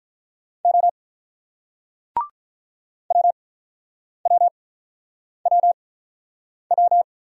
Morse code, heard as a keyed tone, about 47 wpm, spaced as for 9 wpm, 690 Hz. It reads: O WWWW